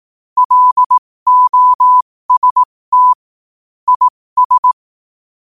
Morse code, heard as a keyed tone, about 18 wpm, spaced as for 15 wpm, 1000 Hz. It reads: LOST IS
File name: qotdfiles / 15wpm